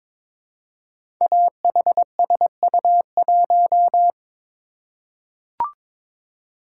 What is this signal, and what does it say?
Morse code, heard as a keyed tone, about 22 wpm, 700 Hz. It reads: AHSU1
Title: Code Groups - 5 Letter:Numbers - 22wpm